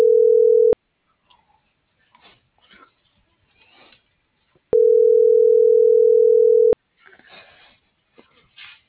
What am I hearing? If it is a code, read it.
Ringback tone (North America), 440 + 480 Hz.